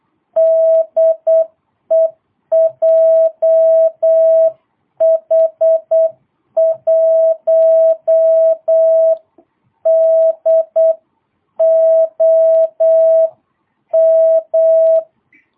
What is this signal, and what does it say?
Morse code, heard as a keyed tone, about 8 wpm, 650 Hz. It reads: DE JH1DOM
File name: jh8cbh / 2022cw